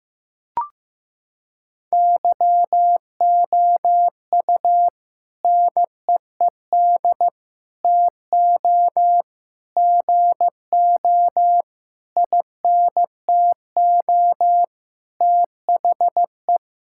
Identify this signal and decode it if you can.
Morse code, heard as a keyed tone, about 15 wpm, 700 Hz. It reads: YOU NEED TO GO INTO THE
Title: Sentences from Top 1000 Words - Review Part 6 - 15wpm